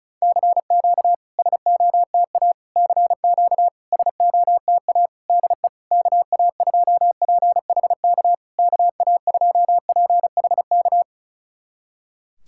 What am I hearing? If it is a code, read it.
Morse code, heard as a keyed tone, about 36 wpm, 700 Hz. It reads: CQ SOTA CQ SOTA DE KA2PHK KA2PHK
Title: SOTA QSO Practice 35wpm